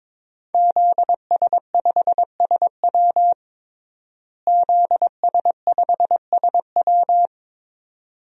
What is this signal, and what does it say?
Morse code, heard as a keyed tone, about 22 wpm, 700 Hz. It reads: ZS5SW ZS5SW
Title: DXCC-100 Call Signs - 5 Characters - Repeated 2x - 22wpm